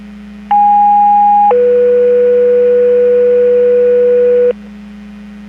Two-tone page, heard as a one-second tone, then a three-second tone, about 820 then 490 Hz, with noise. It is heard from 0.5 to 4.5 s.
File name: Portage 154.385 Paging